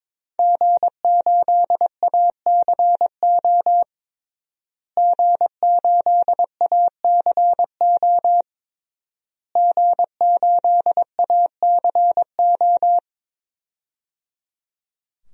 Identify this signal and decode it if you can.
Morse code, heard as a keyed tone, about 21 wpm, 700 Hz. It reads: G8ACO G8ACO G8ACO
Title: DXCC-100 Call Signs - 5 Characters - Repeated 3x - 22wpm